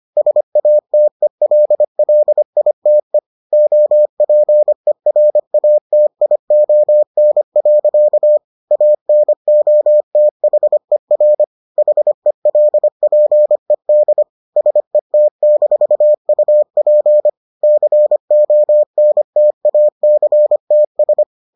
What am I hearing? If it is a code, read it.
Morse code, heard as a keyed tone, about 25 wpm, 600 Hz. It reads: SATELLITE OPERATION. ANOTHER HELPED SET-UP CONTACTS